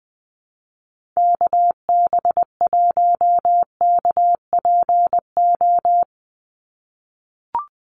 Morse code, heard as a keyed tone, about 20 wpm, 700 Hz. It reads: KB1KPO
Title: USA Call Signs 20wpm